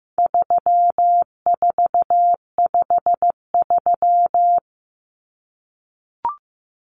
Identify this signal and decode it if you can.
Morse code, heard as a keyed tone, about 15 wpm, 700 Hz. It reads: 3453
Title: Numbers - 4-Digit 15wpm